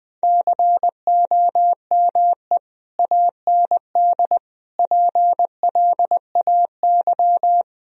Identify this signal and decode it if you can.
Morse code, heard as a keyed tone, about 21 wpm, 700 Hz. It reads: COME AND PLAY
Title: Sentences from Top 1000 Words - Review Part 4 - 20wpm